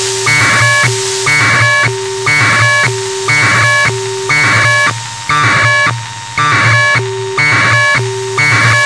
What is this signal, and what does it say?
Packet radio (AX.25, as in APRS).
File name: Sohla-1 23h48 utc 1200bd packet NormalModeby ja0caw